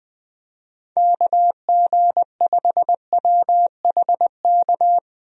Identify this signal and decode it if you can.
Morse code, heard as a keyed tone, about 21 wpm, 700 Hz. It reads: KG5WHK
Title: USA Call Signs - 6 Characters - Repeated 2x - 20wpm